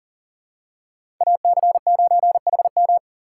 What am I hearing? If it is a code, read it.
Morse code, heard as a keyed tone, about 37 wpm, 700 Hz. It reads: AC9HM
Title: USA Call Signs - 3x Character Spacing - 40wpm